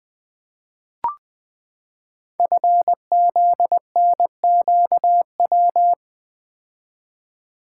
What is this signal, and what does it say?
Morse code, heard as a keyed tone, about 20 wpm, 700 Hz. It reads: FZNQW
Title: Code Groups - 5 Letters - 20wpm